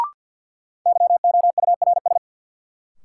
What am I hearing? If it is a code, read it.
Morse code, heard as a keyed tone, about 48 wpm, 700 Hz. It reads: YOURS
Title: Sentences from Top 1500 Words - New Word Drill - 50wpm